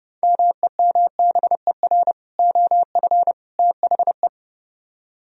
Morse code, heard as a keyed tone, about 30 wpm, 700 Hz. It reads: MEMBER OF THE